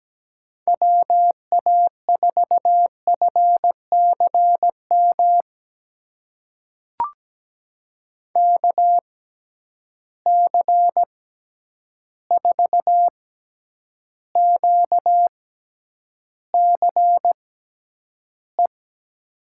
Morse code, heard as a keyed tone, about 17 wpm, spaced as for 9 wpm, 700 Hz. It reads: WA4FCM KC4QCE
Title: USA Call Signs - 6 Characters - 6x Character Spacing - 17wpm